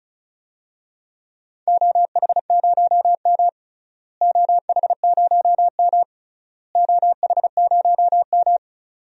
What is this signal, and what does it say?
Morse code, heard as a keyed tone, about 34 wpm, 700 Hz. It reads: OH0M OH0M OH0M
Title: DXCC-100 Call Signs - 4 Characters - Repeated 3x - 35wpm